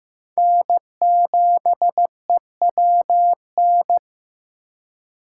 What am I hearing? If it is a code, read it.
Morse code, heard as a keyed tone, about 18 wpm, 700 Hz. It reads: N7EWN
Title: Speed Racing - USA 1x3 Call Signs 15wpm